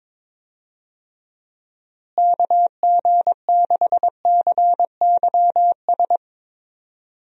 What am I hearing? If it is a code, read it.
Morse code, heard as a keyed tone, about 22 wpm, 700 Hz. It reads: KG6CYS
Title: USA Call Signs - 6 Characters - Repeated 2x - 22wpm